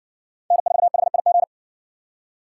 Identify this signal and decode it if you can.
Morse code, heard as a keyed tone, about 60 wpm, 690 Hz. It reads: N4HIG